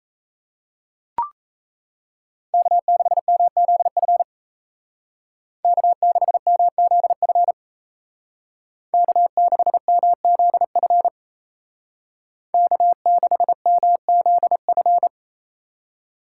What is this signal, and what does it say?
Morse code, heard as a keyed tone, about 33 wpm, 700 Hz. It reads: K6MZF K6MZF K6MZF K6MZF
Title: Speed Racing - USA 1x3 Call Signs 28wpm